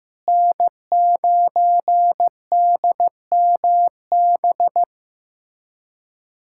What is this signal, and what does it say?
Morse code, heard as a keyed tone, about 15 wpm, 700 Hz. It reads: N9DMB
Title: USA Callsign DE Callsign - 5 Characters - 15wpm